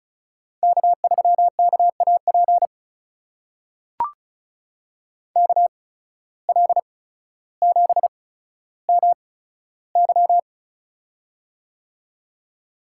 Morse code, heard as a keyed tone, about 34 wpm, spaced as for 15 wpm, 700 Hz. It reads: K3KAP KL7MY
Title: USA Call Signs - 5 Characters - 8x Character Spacing - 35wpm